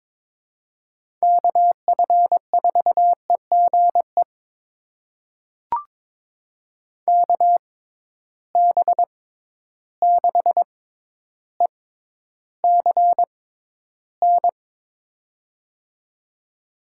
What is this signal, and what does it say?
Morse code, heard as a keyed tone, about 22 wpm, spaced as for 11 wpm, 700 Hz. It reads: KF4EGE KB6ECN
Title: USA Call Signs - 6 Characters - 6x Character Spacing - 22wpm